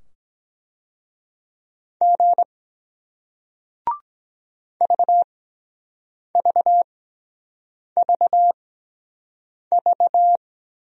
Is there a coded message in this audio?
Yes: Morse code from 2.0 to 10.4 s, heard as a keyed tone, about 23 wpm, spaced as for 7 wpm, 700 Hz. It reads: G VVVV